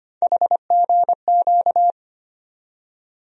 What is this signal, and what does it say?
Morse code, heard as a keyed tone, about 25 wpm, 700 Hz. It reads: HGQ
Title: USA Call Sign Postfixes - 3 Letters 25wpm